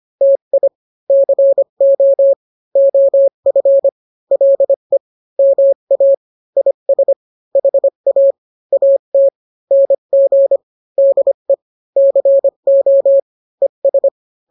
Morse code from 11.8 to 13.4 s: CO